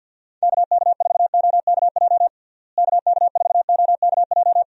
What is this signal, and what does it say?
Morse code, heard as a keyed tone, about 53 wpm, 700 Hz. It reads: KK4OKJ KK4OKJ
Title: USA Call Signs - 6 Characters - Repeated 2x - 50wpm